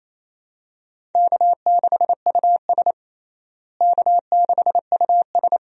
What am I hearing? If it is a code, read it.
Morse code, heard as a keyed tone, about 28 wpm, 700 Hz. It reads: K6US K6US
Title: USA Call Signs - 4 Characters - Repeated 2x - 28wpm